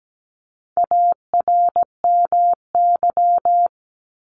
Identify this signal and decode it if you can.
Morse code, heard as a keyed tone, about 17 wpm, 700 Hz. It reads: ARMY
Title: Top 501-750 Words 17wpm